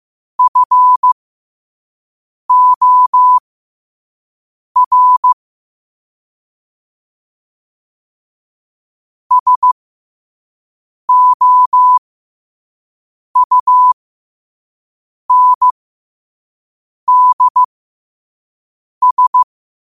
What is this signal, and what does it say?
Morse code, heard as a keyed tone, about 15 wpm, spaced as for 5 wpm, 1000 Hz. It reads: FOR SOUNDS
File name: 5wpm